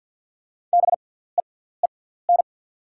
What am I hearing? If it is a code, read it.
Morse code, heard as a keyed tone, about 48 wpm, spaced as for 17 wpm, 690 Hz. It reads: BEEN
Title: 6x Character Spacing - Top 100 Words - 50wpm